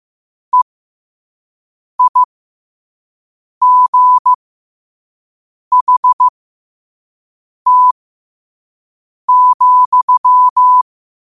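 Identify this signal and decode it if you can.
Morse code, heard as a keyed tone, about 15 wpm, spaced as for 5 wpm, 1000 Hz. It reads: EIGHT,
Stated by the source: Morse quote-of-the-day practice at 5 wpm.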